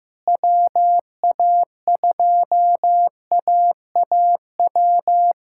Morse code, heard as a keyed tone, about 16 wpm, 700 Hz. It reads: WA2AAW
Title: K1USN SST QSO Practice - 15wpm